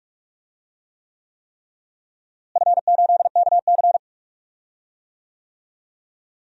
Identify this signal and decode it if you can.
Morse code, heard as a keyed tone, about 43 wpm, 700 Hz. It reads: F8KC
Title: DXCC-100 Call Signs - 4 Characters - Repeated 3x - 45wpm